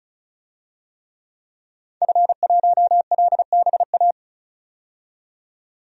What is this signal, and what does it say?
Morse code, heard as a keyed tone, about 36 wpm, 700 Hz. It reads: F1LBA
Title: DXCC-100 Call Signs - 5 Characters - 35wpm